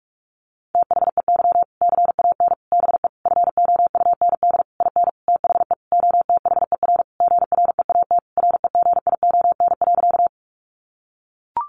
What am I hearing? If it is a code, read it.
Morse code, heard as a keyed tone, about 44 wpm, 700 Hz. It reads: THEY CAN BE FOUND IN THE OTHER GREAT REGION.
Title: Sentences from Top 1500 Words 45wpm